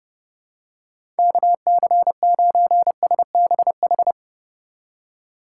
Morse code, heard as a keyed tone, about 32 wpm, 700 Hz. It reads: KC9SBH
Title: USA Call Signs - 6 Characters - 4x Character Spacing - 30wpm